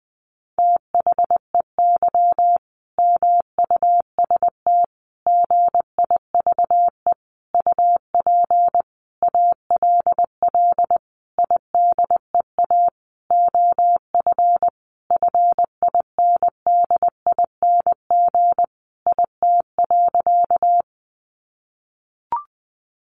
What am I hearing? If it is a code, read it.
Morse code, heard as a keyed tone, about 20 wpm, 700 Hz. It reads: THEY MUST GIVE UP ALL IDEA OF FINDING IT.
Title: Sentences from Top 1100 Words 20wpm